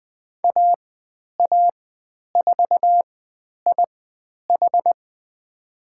Morse code, heard as a keyed tone, about 20 wpm, spaced as for 10 wpm, 700 Hz. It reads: AA4IH
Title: 102 Lesson 17g - Letter u - Call Signs 20wpm